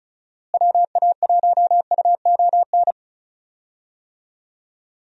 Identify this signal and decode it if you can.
Morse code, heard as a keyed tone, about 34 wpm, 700 Hz. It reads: WA1UON
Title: USA Call Signs - 6 Characters - Repeated 2x - 35wpm